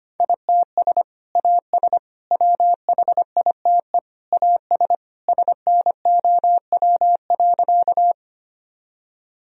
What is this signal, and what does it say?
Morse code, heard as a keyed tone, about 25 wpm, 700 Hz. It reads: ITS AS WHITE AS SNOW.
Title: Sentences from Top 1000 Words - Review Part 3 - 25wpm